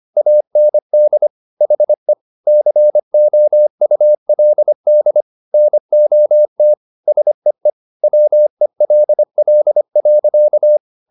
Morse code, heard as a keyed tone, about 25 wpm, 600 Hz. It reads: AND HE COULD NOT SEE WELL.